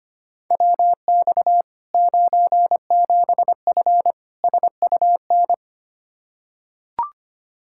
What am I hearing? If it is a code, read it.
Morse code, heard as a keyed tone, about 25 wpm, 700 Hz. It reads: WX 97F SUN
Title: Practice QSO - Exchange 2 - Info - 25wpm